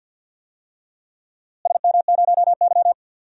Morse code, heard as a keyed tone, about 50 wpm, 700 Hz. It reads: SM0Y
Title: DXCC-100 Call Signs - 4 Characters - Repeated 2x - 50wpm